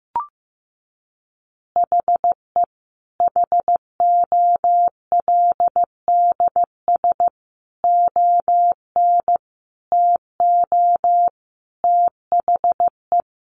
Morse code, heard as a keyed tone, about 15 wpm, 700 Hz. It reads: HE HOLDS ON TO THE
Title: Additional Easy Phrases 15wpm